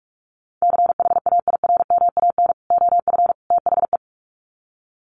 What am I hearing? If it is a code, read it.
Morse code, heard as a keyed tone, about 44 wpm, 700 Hz. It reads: CHAIRMAN OF THE
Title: Sets of 3 Words 45wpm